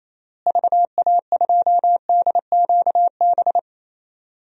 Morse code, heard as a keyed tone, about 28 wpm, 700 Hz. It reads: VA2DQB
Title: DXCC-100 Call Signs - 6 Characters - Repeated 3x - 28wpm